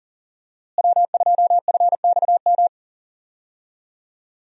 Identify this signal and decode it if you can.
Morse code, heard as a keyed tone, about 40 wpm, 700 Hz. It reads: W2FXM
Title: USA Call Signs - 5 Characters - Repeated 2x - 40wpm